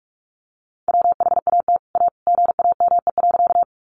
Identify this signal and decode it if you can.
Morse code, heard as a keyed tone, about 44 wpm, 700 Hz. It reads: WHAT A GAME.
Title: Sentences from Top 500 Words 45wpm